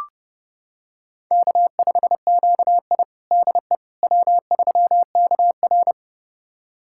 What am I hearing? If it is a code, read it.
Morse code, heard as a keyed tone, about 30 wpm, 700 Hz. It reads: K5QI DE W3KR
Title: USA Callsign DE Callsign - 4 Characters - 30wpm